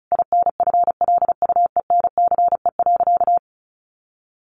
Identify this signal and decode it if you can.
Morse code, heard as a keyed tone, about 34 wpm, 700 Hz. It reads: INFLUENCE.
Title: Sentences from Top 1100 Words 35wpm